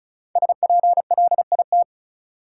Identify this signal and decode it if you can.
Morse code, heard as a keyed tone, about 36 wpm, 700 Hz. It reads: SPLIT
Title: Five Letter Words - 35wpm